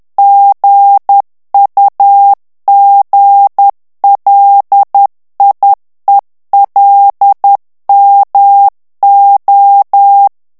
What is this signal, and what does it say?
Morse code, heard as a keyed tone, about 10 wpm, 800 Hz. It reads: GUGLIELMO
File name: morse mp3 / ITALIEN texte clair 1 vitesse 10 mots minute